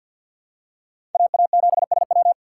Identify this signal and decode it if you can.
Morse code, heard as a keyed tone, about 53 wpm, 700 Hz. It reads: AA7SW
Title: USA Call Signs - 5 Characters - 3x Character Spacing - 50wpm